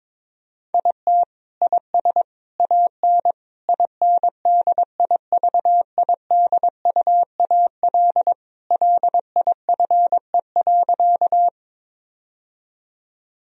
Morse code, heard as a keyed tone, about 22 wpm, 700 Hz. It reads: IT IS AN INDIVIDUAL LIFE.
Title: Sentences from Top 1000 Words - Review Part 4 - 22wpm